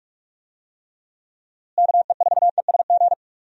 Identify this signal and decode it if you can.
Morse code, heard as a keyed tone, about 47 wpm, 700 Hz. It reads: KE4ESG
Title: USA Call Signs - 6 Characters - Repeated 3x - 45wpm